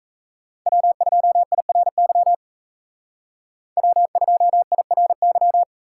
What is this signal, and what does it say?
Morse code, heard as a keyed tone, about 33 wpm, 700 Hz. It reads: W2IRY W2IRY
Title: Speed Racing - USA 1x3 Call Signs 28wpm